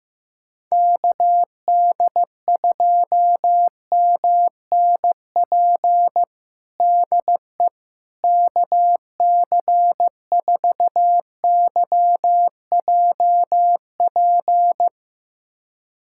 Morse code, heard as a keyed tone, about 15 wpm, 700 Hz. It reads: KD2MNP DE KC4YJP
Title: USA Callsign DE Callsign 15wpm